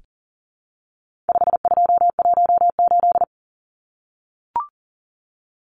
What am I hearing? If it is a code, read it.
Morse code, heard as a keyed tone, about 40 wpm, 700 Hz. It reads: 5218
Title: Numbers - 4-Digit 40wpm